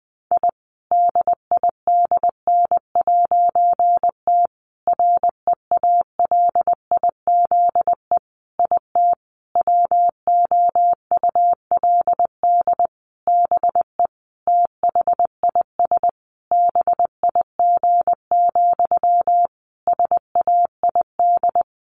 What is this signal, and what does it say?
Morse code, heard as a keyed tone, about 20 wpm, 700 Hz. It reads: I DIDN'T REALIZE IT WOULD BE THIS BIG, SAID
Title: A Sound Of Thunder 20wpm